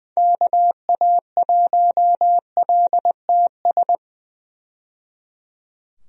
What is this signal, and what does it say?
Morse code, heard as a keyed tone, about 20 wpm, 700 Hz. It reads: KA1LTS
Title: USA Call Signs - 6 Characters - 20wpm